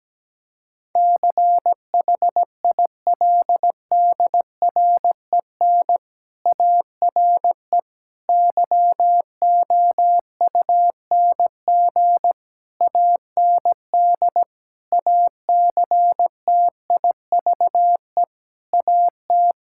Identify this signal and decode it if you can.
Morse code, heard as a keyed tone, about 17 wpm, 700 Hz. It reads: CHILDREN ARE YOUNG AND ACTIVE AT
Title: Sentences from Top 1000 Words - Review Part 6 - 17wpm